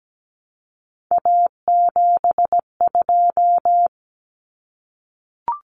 Morse code, heard as a keyed tone, about 17 wpm, 700 Hz. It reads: A72
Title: Common Cut Numbers - 3-Digit 17wpm